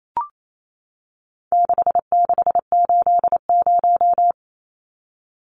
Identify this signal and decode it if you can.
Morse code, heard as a keyed tone, about 28 wpm, 700 Hz. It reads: 6680
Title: Numbers - 4-Digit 28wpm